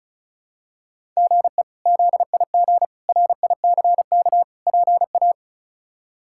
Morse code, heard as a keyed tone, about 36 wpm, 700 Hz. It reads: GE ZIG RICK PA
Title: K1USN SST QSO Practice - 35wpm